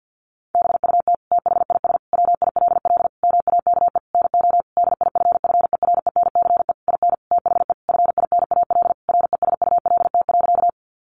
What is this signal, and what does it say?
Morse code, heard as a keyed tone, about 50 wpm, 700 Hz. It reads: BUT THIS WILL MAKE NO DIFFERENCE IN THE FINAL RESULT.
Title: Sentences from Top 800 Words 50wpm